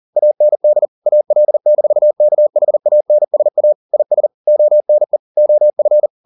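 Morse code, heard as a keyed tone, about 40 wpm, 600 Hz. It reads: AND AL-KHANSA IS ONE OF
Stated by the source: Morse code at 40 wpm, practice text, CW med Gnister.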